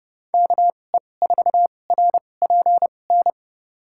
Morse code, heard as a keyed tone, about 30 wpm, spaced as for 22 wpm, 700 Hz. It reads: KE4RPN
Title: USA Call Signs - 6 Characters - 2x Character Spacing - 30wpm